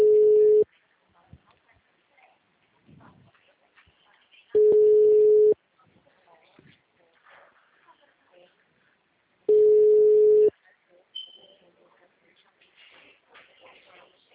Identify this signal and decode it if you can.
Ringback tone (Europe, 425 Hz), with speech.